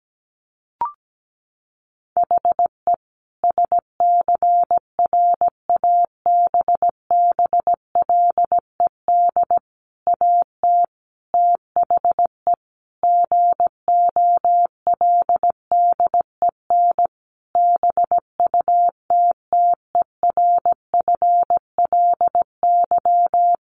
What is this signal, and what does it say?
Morse code, heard as a keyed tone, about 17 wpm, 700 Hz. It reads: HE SCRABBLED AT THE GOLDEN BUTTERFLY
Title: A Sound Of Thunder 17wpm